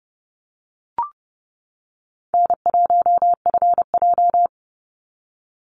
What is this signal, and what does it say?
Morse code, heard as a keyed tone, about 30 wpm, 700 Hz. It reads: N1FJ
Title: NA SOTA Chaser Callsigns 30wpm